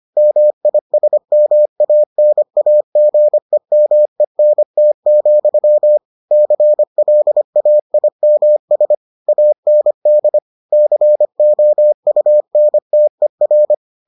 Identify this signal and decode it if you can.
Morse code, heard as a keyed tone, about 25 wpm, 600 Hz. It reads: MISMANAGEMENT, CLAIMS AND COUNTER